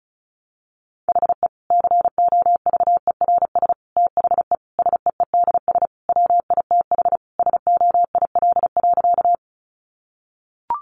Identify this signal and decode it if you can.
Morse code, heard as a keyed tone, about 34 wpm, 700 Hz. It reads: HE COVERS THE SEEDS WITH SOIL.
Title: Easy Phrases 35wpm